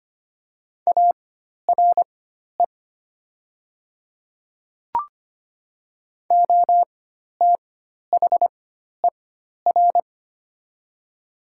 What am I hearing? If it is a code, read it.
Morse code, heard as a keyed tone, about 25 wpm, spaced as for 12 wpm, 700 Hz. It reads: ARE OTHER
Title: 4x Character Spacing - Top 300 Words - 25wpm